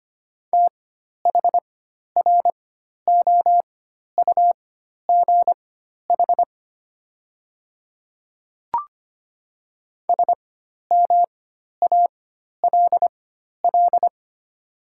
Morse code, heard as a keyed tone, about 25 wpm, spaced as for 12 wpm, 700 Hz. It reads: THROUGH SMALL
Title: 4x Character Spacing - Top 200 Words - 25wpm